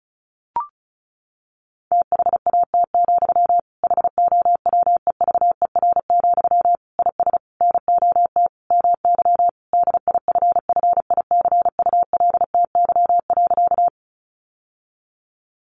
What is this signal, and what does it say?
Morse code, heard as a keyed tone, about 34 wpm, 700 Hz. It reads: THAT, HOWEVER, IS NOT MY DIFFICULTY.
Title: Sentences from Top 1500 Words 35wpm